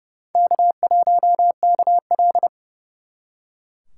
Morse code, heard as a keyed tone, about 36 wpm, 700 Hz. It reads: K1KL
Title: Speed Racing - USA 1x2 Call Signs 30wpm